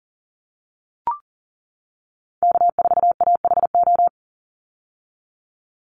Morse code, heard as a keyed tone, about 40 wpm, 700 Hz. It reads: K4AHO
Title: Call Signs - 40wpm